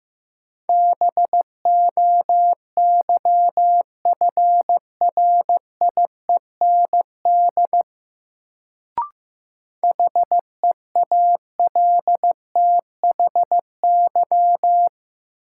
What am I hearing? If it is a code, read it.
Morse code, heard as a keyed tone, about 15 wpm, 700 Hz. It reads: BOYFRIEND HEALTHY
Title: Sentences from Top 1000 Words - Review Part 5 - New Word Drill - 15wpm